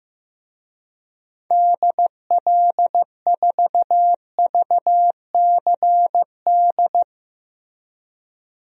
Morse code, heard as a keyed tone, about 15 wpm, 700 Hz. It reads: DL4VCD
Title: DXCC-100 Call Signs - 6 Characters - Repeated 2x - 15wpm